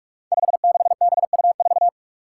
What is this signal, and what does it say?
Morse code, heard as a keyed tone, about 46 wpm, 700 Hz. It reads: 56BU4
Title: Five Challenging Letter-Number Combinations 45wpm